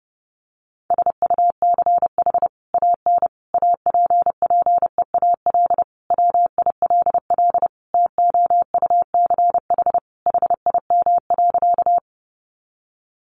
Morse code, heard as a keyed tone, about 30 wpm, 700 Hz. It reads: SUCH AN APPEAL WILL TOUCH HIM.
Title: Sentences from Top 1300 Words 30wpm